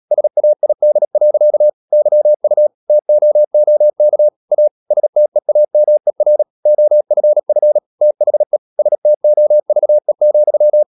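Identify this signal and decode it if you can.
Morse code, heard as a keyed tone, about 37 wpm, 600 Hz. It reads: SAID. YU TOOK A STEAMER OFF THE STOVE,
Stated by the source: Morse code practice at 35–39 wpm sending English text of Ken Follett's novel Never, from CW med Gnister.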